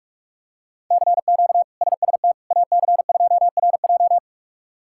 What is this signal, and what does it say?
Morse code, heard as a keyed tone, about 46 wpm, 700 Hz. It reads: CQ SST AC2RJ
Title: K1USN SST QSO Practice - 45wpm